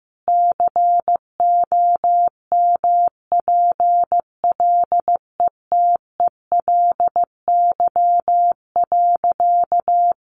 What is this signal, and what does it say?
Morse code, heard as a keyed tone, about 15 wpm, 700 Hz. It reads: COMPLETELY.
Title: Sentences from Top 1100 Words 15wpm